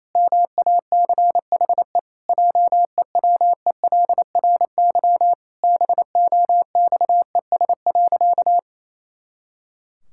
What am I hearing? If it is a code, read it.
Morse code, heard as a keyed tone, about 28 wpm, 700 Hz. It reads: MACHE JEWELRY BOXES.
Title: Pangram Challenge v2 - 28wpm